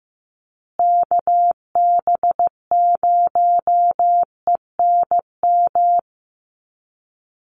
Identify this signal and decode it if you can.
Morse code, heard as a keyed tone, about 15 wpm, 700 Hz. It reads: KB0ENM
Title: USA Call Signs Repeated 2x 15wpm